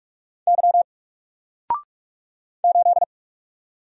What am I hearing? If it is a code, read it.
Morse code, heard as a keyed tone, about 44 wpm, 700 Hz. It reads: Y 8
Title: Single Character (Letter:Number:Punctuation:Prosign) Alt 45wpm